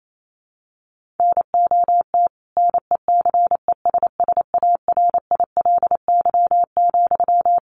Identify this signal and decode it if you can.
Morse code, heard as a keyed tone, about 28 wpm, 700 Hz. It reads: NOT NECESSARILY,
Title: Sentences from Top 1400 Words 28wpm